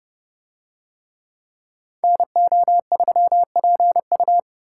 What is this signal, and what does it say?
Morse code, heard as a keyed tone, about 29 wpm, 700 Hz. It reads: NO3PU